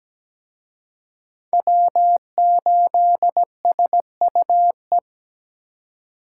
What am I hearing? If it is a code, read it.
Morse code, heard as a keyed tone, about 20 wpm, 700 Hz. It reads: W8SUE
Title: Speed Racing - USA Call Signs 17wpm